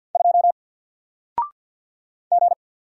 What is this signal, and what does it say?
Morse code, heard as a keyed tone, about 50 wpm, 700 Hz. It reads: 2 G
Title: Single Letter-Number 50wpm